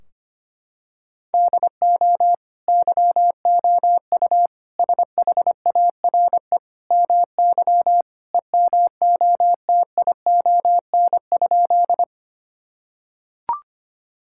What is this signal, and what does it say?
Morse code, heard as a keyed tone, about 25 wpm, 700 Hz. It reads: DO YOU SHARE MY EMOTION?
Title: Sentences from Top 1600 Words 25wpm